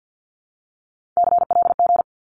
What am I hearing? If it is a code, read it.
Morse code, heard as a keyed tone, about 48 wpm, 700 Hz. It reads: CLG
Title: Abbreviations Repeated 3x - 50wpm